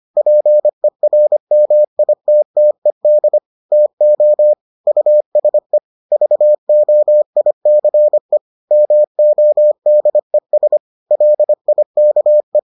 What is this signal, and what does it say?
Morse code, heard as a keyed tone, about 25 wpm, 600 Hz. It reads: PERMITTED TO USE VOICE MODES LIKE